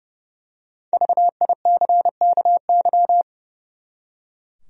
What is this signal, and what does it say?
Morse code, heard as a keyed tone, about 30 wpm, 700 Hz. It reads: VICKY
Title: USA Common Names - 30wpm